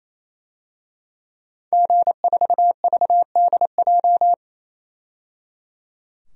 Morse code, heard as a keyed tone, about 28 wpm, 700 Hz. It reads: G4VDJ
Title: DXCC-100 Call Signs - 5 Characters - 28wpm